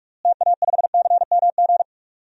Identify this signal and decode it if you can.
Morse code, heard as a keyed tone, about 46 wpm, 700 Hz. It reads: TA5CMG
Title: DXCC-100 Call Signs - 6 Characters - Repeated 2x - 45wpm